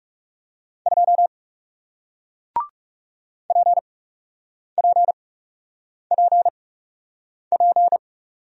Morse code, heard as a keyed tone, about 41 wpm, spaced as for 8 wpm, 700 Hz. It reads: 2 PPPP